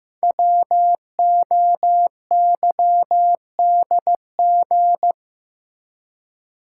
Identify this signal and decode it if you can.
Morse code, heard as a keyed tone, about 15 wpm, 700 Hz. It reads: WOYDG
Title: Code Groups - 5 Letters - 15wpm